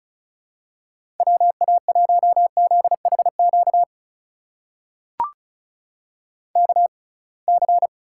Morse code, heard as a keyed tone, about 36 wpm, spaced as for 27 wpm, 700 Hz. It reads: WA1ZHQ KC
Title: USA Call Signs - 6 Characters - 6x Character Spacing - 35wpm